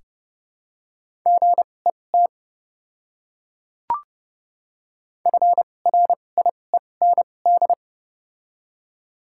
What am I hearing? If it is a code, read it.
Morse code, heard as a keyed tone, about 30 wpm, spaced as for 22 wpm, 700 Hz. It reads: GET FRIEND
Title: 2x Character Spacing - Top 300 Words - 30wpm